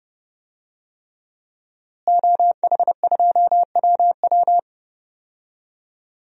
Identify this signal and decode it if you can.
Morse code, heard as a keyed tone, about 32 wpm, 700 Hz. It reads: OH2WW
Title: DXCC-100 Call Signs - 5 Characters - Repeated 3x - 30wpm